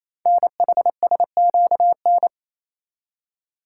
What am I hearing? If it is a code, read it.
Morse code, heard as a keyed tone, about 27 wpm, 700 Hz. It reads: NHSQN
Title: Code Groups - 5 Letter:Numbers - 28wpm